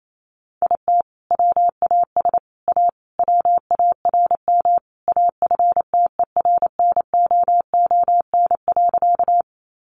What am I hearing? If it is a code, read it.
Morse code, heard as a keyed tone, about 28 wpm, 700 Hz. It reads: IT WAS A WARM AFTERNOON.
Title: Sentences from Top 1500 Words 28wpm